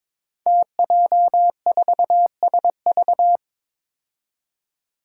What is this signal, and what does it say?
Morse code, heard as a keyed tone, about 23 wpm, 700 Hz. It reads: TJ4SV
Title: Five Challenging Letter-Number Combinations 22wpm